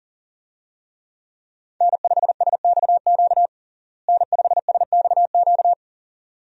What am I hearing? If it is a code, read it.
Morse code, heard as a keyed tone, about 40 wpm, 700 Hz. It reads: N5SXQ N5SXQ
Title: USA Call Signs - 5 Characters - Repeated 2x - 40wpm